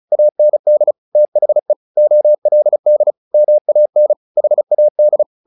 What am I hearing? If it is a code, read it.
Morse code, heard as a keyed tone, about 34 wpm, 600 Hz. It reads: AND THE OLD MAN HAD
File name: the old man and the sea / the old man and the sea 1 35wpm